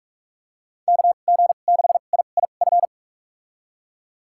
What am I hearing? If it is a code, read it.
Morse code, heard as a keyed tone, about 46 wpm, spaced as for 32 wpm, 700 Hz. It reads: KG6IIF
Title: USA Call Signs - 2x Character Spacing - 45wpm